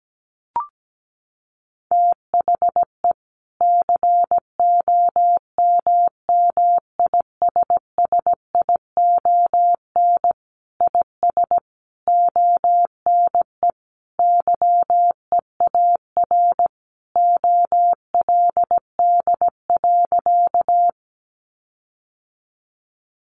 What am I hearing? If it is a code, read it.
Morse code, heard as a keyed tone, about 17 wpm, 700 Hz. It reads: THE COMMISSION IS ONE YEAR OLD.
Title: Sentences from Top 1400 Words 17wpm